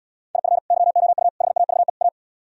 Morse code, heard as a keyed tone, about 77 wpm, 700 Hz. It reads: IF YOU HAVE A